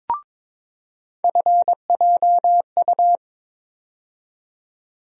Morse code, heard as a keyed tone, about 22 wpm, 700 Hz. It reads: FJU
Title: USA Call Sign Postfixes - 3 Letters 22wpm